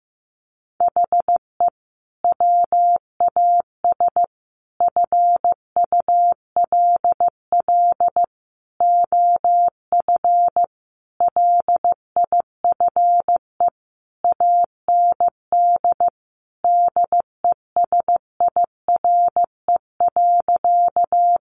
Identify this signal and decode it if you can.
Morse code, heard as a keyed tone, about 15 wpm, 700 Hz. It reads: HE WAS FULL OF LIFE AND DESIRE.
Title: Sentences from Top 1100 Words 15wpm